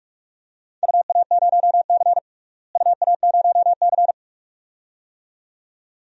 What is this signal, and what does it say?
Morse code, heard as a keyed tone, about 47 wpm, 700 Hz. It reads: UA0C UA0C
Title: DXCC-100 Call Signs - 4 Characters - Repeated 2x - 45wpm